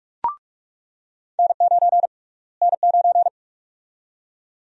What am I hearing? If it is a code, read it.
Morse code, heard as a keyed tone, about 45 wpm, 700 Hz. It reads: N9 N9
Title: USA Call Sign Prefixes - Repeated 2x - 45wpm